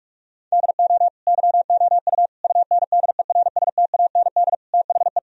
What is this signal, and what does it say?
Morse code, heard as a keyed tone, about 47 wpm, 700 Hz. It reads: DO YOU UNDERSTAND THE
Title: Sentences from Top 1000 Words - Review Part 3 - 45wpm